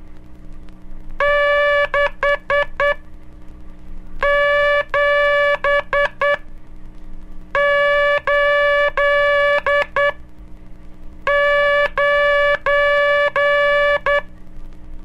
Morse code, heard as a keyed tone, about 10 wpm, spaced as for 5 wpm, 580 Hz, with noise. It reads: 6789
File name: cursos / Curso CW mp3 / CW Aula - 11